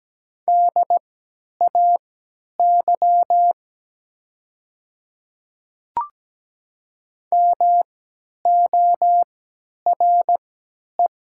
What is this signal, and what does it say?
Morse code, heard as a keyed tone, about 17 wpm, spaced as for 10 wpm, 700 Hz. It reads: DAY MORE
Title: 3x Character Spacing - Top 100 Words - 17wpm